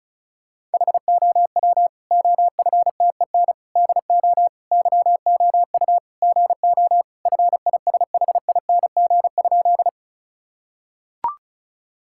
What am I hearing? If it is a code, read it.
Morse code, heard as a keyed tone, about 34 wpm, 700 Hz. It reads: HOW OFTEN DO YOU GO FISHING?
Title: Sentences from Top 1000 Words - Review Part 5 - 35wpm